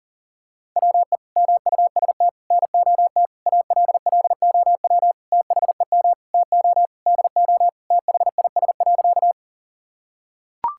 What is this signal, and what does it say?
Morse code, heard as a keyed tone, about 40 wpm, 700 Hz. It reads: WE MUST NOT ALLOW THEM TO DO THIS.
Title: Sentences from Top 1000 Words - Review Part 5 - 40wpm